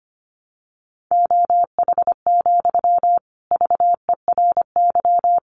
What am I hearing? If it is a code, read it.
Morse code, heard as a keyed tone, about 25 wpm, 700 Hz. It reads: OH, VERY
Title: Sentences from Top 1000 Words 25wpm